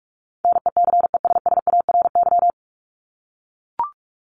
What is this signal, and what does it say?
Morse code, heard as a keyed tone, about 44 wpm, 700 Hz. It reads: NECESSARY
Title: LY Suffix - Top 100 Words with Base Words 45wpm